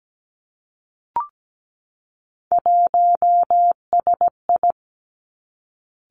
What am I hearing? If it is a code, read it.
Morse code, heard as a keyed tone, about 17 wpm, 700 Hz. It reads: 1SI
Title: Three Challenging Letter-Number Combinations 17wpm